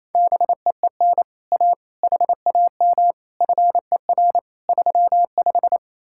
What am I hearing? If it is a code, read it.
Morse code, heard as a keyed tone, about 37 wpm, 700 Hz. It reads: BEEN A HAM FER 35